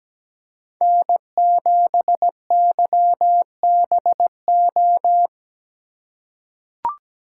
Morse code, heard as a keyed tone, about 17 wpm, 700 Hz. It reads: N7YBO
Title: USA Call Signs - 5 Characters - 2x Character Spacing - 17wpm